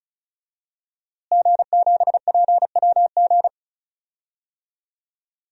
Morse code, heard as a keyed tone, about 36 wpm, 700 Hz. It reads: G7PWG
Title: DXCC-100 Call Signs - 5 Characters - Repeated 3x - 35wpm